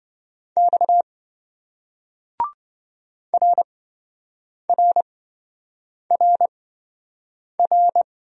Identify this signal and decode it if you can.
Morse code, heard as a keyed tone, about 27 wpm, spaced as for 7 wpm, 700 Hz. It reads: X RRRR